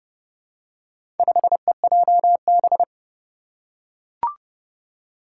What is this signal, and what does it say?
Morse code, heard as a keyed tone, about 30 wpm, 700 Hz. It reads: 5EJB
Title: Four Challenging Letter-Number Combinations 30wpm